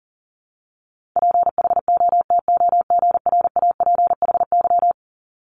Morse code, heard as a keyed tone, about 37 wpm, 700 Hz. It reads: PHOTOGRAPHY
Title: Multi-letter Phonemes Repeated 3x 40wpm